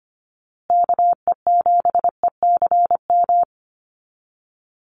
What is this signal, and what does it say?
Morse code, heard as a keyed tone, about 25 wpm, 700 Hz. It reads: KE7ECM
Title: USA Call Signs Repeated 2x 25wpm